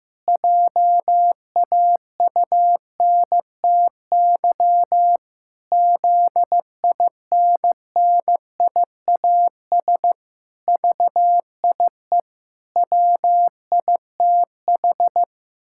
Morse code, heard as a keyed tone, about 15 wpm, 700 Hz. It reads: JAUNTY ZINNIAS VIE WITH
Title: Pangram Challenge v2 - 15wpm